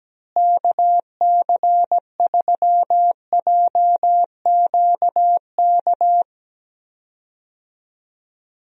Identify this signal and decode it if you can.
Morse code, heard as a keyed tone, about 17 wpm, 700 Hz. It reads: KC3JQK
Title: USA Call Signs - 6 Characters - Repeated 3x - 17wpm